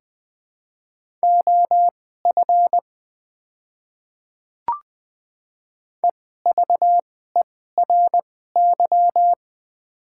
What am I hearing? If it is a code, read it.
Morse code, heard as a keyed tone, about 20 wpm, spaced as for 15 wpm, 700 Hz. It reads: OF EVERY
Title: 2x Character Spacing - Top 200 Words - 20wpm